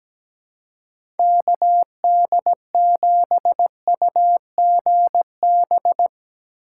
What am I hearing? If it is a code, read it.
Morse code, heard as a keyed tone, about 17 wpm, 700 Hz. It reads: KD7UGB
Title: USA Call Signs - 6 Characters - 2x Character Spacing - 17wpm